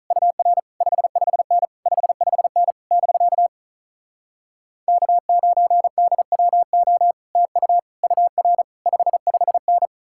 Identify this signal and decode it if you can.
Morse code, heard as a keyed tone, about 47 wpm, 700 Hz. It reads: UR 55N 55N <BK> K9DWO TU UR 55N